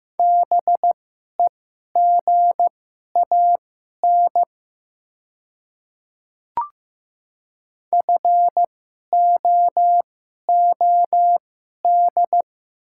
Morse code, heard as a keyed tone, about 15 wpm, spaced as for 11 wpm, 700 Hz. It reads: BEGAN FOOD
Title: 2x Character Spacing - Top 300 Words - 15wpm